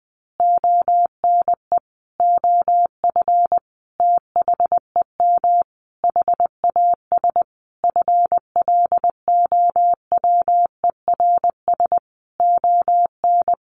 Morse code, heard as a keyed tone, about 20 wpm, 700 Hz. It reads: ONE OF THEM HAS FLOWERS ON
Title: Additional Easy Phrases 20wpm